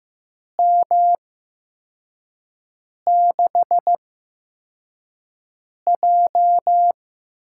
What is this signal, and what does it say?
Morse code, heard as a keyed tone, about 15 wpm, spaced as for 4 wpm, 700 Hz. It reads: M6J
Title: USA Call Signs - 6 Characters - 8x Character Spacing - 15wpm